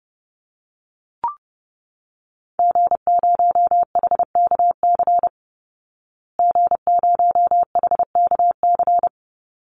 Morse code, heard as a keyed tone, about 30 wpm, 700 Hz. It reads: G0HKC G0HKC
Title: Call Signs Repeated 2x - 30wpm